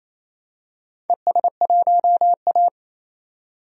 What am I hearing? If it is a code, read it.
Morse code, heard as a keyed tone, about 27 wpm, 690 Hz. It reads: ES1A
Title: DXCC-100 Call Signs - 4 Characters - 28wpm